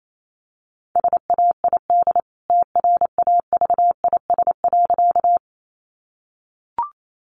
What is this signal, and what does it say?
Morse code, heard as a keyed tone, about 28 wpm, 700 Hz. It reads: SAID TRAVIS.
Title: A Sound Of Thunder 28wpm